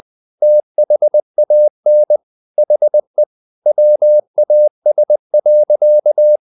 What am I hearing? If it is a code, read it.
Morse code, heard as a keyed tone, about 20 wpm, 600 Hz. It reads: THAN HE WAS.
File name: the old man and the sea part 2 20wpm